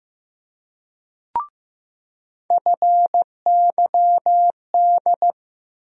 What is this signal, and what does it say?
Morse code, heard as a keyed tone, about 15 wpm, 700 Hz. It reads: FYD
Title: USA Call Sign Postfixes - 3 Letters 15wpm